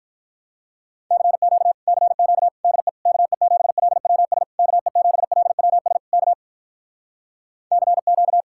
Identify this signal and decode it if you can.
Morse code, heard as a keyed tone, about 46 wpm, 700 Hz. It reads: CQ CQ DE KE7LWS KE7LWS K CQ